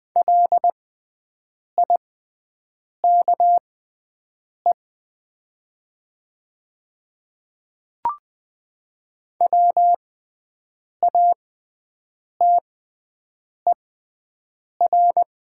Morse code, heard as a keyed tone, about 20 wpm, spaced as for 7 wpm, 700 Hz. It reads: LIKE WATER
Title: 6x Character Spacing - Top 500 Words - 20wpm